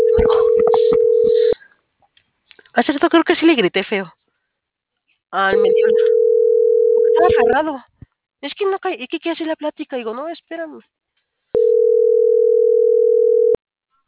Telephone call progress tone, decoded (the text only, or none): ringback tone